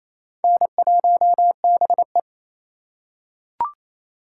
Morse code, heard as a keyed tone, about 27 wpm, 700 Hz. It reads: N1BE
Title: USA Call Signs - 4 Characters - 4x Character Spacing - 28wpm